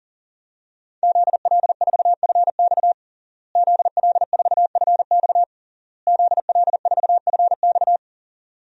Morse code, heard as a keyed tone, about 40 wpm, 700 Hz. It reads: ZL4FX ZL4FX ZL4FX
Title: DXCC-100 Call Signs - 5 Characters - Repeated 3x - 40wpm